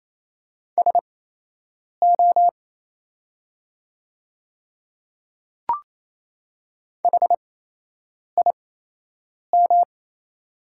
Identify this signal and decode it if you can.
Morse code, heard as a keyed tone, about 28 wpm, spaced as for 8 wpm, 700 Hz. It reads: SO HIM
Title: 8x Character Spacing - Top 100 Words - 28wpm